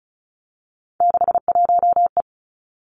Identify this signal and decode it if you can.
Morse code, heard as a keyed tone, about 36 wpm, 700 Hz. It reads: 61E
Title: Three Challenging Letter-Number Combinations 35wpm